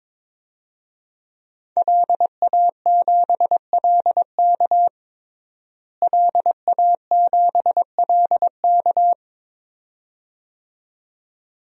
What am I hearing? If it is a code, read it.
Morse code, heard as a keyed tone, about 22 wpm, 700 Hz. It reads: LA7LK LA7LK
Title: DXCC-100 Call Signs - 5 Characters - Repeated 2x - 22wpm